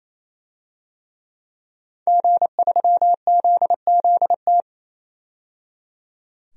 Morse code, heard as a keyed tone, about 27 wpm, 700 Hz. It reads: G3ZZT
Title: DXCC-100 Call Signs - 5 Characters - 28wpm